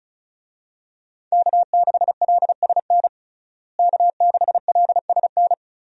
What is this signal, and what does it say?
Morse code, heard as a keyed tone, about 34 wpm, 700 Hz. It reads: K6LSN K6LSN
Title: USA Call Signs - 5 Characters - Repeated 2x - 35wpm